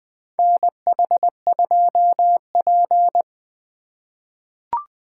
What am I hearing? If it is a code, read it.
Morse code, heard as a keyed tone, about 20 wpm, 700 Hz. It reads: NH2P
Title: USA Call Signs - 4 Characters - 3x Character Spacing - 20wpm